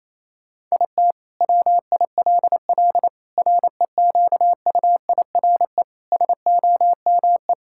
Morse code, heard as a keyed tone, about 28 wpm, 700 Hz. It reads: IT WILL REQUIRE SOME
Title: Sentences from Top 1000 Words - Review Part 6 - 28wpm